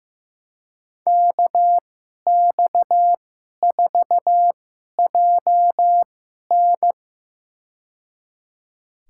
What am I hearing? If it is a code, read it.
Morse code, heard as a keyed tone, about 15 wpm, spaced as for 11 wpm, 700 Hz. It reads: KX4JN
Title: USA Call Signs - 5 Characters - 2x Character Spacing - 15wpm